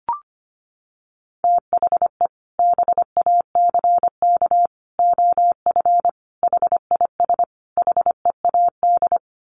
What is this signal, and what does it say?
Morse code, heard as a keyed tone, about 25 wpm, 700 Hz. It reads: THE BACK OF HIS HEAD
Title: Sets of 5 Words Encore 25wpm